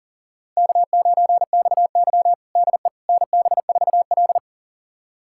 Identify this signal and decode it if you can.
Morse code, heard as a keyed tone, about 40 wpm, 700 Hz. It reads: K9XY DE NB4L
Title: USA Callsign DE Callsign - 4 Characters - 40wpm